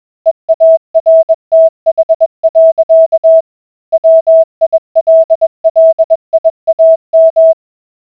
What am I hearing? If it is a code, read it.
Morse code, heard as a keyed tone, about 20 wpm, 650 Hz. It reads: EARTH. WILLIAM